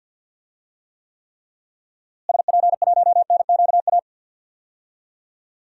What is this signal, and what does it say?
Morse code, heard as a keyed tone, about 50 wpm, 700 Hz. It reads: SP1NQA
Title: DXCC-100 Call Signs - 6 Characters - Repeated 3x - 50wpm